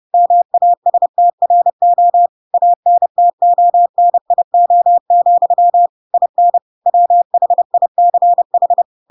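Morse code, heard as a keyed tone, about 30 wpm, 700 Hz. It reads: MASTRO ANTONIO, IN WHICH
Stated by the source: Chapter 2 of The Adventures of Pinocchio read as a book in Morse code (CW) at 30 wpm, by ik1hge.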